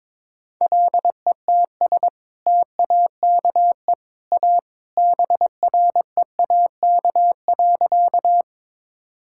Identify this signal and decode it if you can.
Morse code, heard as a keyed tone, about 22 wpm, 700 Hz. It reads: LETS TAKE A BREAK.
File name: practice / 22wpm / Sentences with 2-Word Phrases - 22wpm